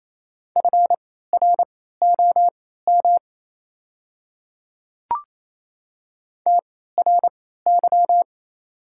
Morse code, heard as a keyed tone, about 28 wpm, spaced as for 16 wpm, 690 Hz. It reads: FROM TRY
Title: 3x Character Spacing - Top 200 Words - 28wpm